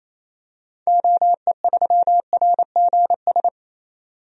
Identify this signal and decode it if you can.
Morse code, heard as a keyed tone, about 28 wpm, 700 Hz. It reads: OE3RGS
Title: DXCC-100 Call Signs - 6 Characters - 28wpm